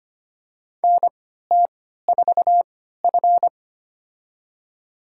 Morse code, heard as a keyed tone, about 25 wpm, spaced as for 14 wpm, 700 Hz. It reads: NT4F
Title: USA Call Signs - 4 Characters - 3x Character Spacing - 25wpm